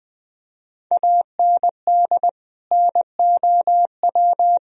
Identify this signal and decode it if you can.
Morse code, heard as a keyed tone, about 20 wpm, 700 Hz. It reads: AND NOW
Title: Sentences from Top 1000 Words - Review Part 1 - 20wpm